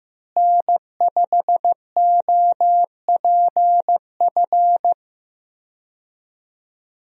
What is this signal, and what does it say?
Morse code, heard as a keyed tone, about 15 wpm, 700 Hz. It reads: N5OPF
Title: USA Call Signs - 5 Characters - Repeated 2x - 15wpm